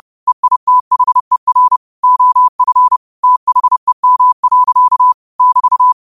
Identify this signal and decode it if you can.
Morse code, heard as a keyed tone, about 30 wpm, 1000 Hz. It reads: EITHER OF THEM. <BT>
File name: qotdfiles / 30wpm